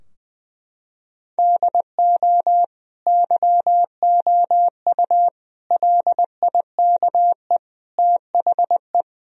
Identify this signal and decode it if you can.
Morse code, heard as a keyed tone, about 20 wpm, 700 Hz. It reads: DO YOU LIKE THE
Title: Sentences from Top 1000 Words - Review Part 2 - 20wpm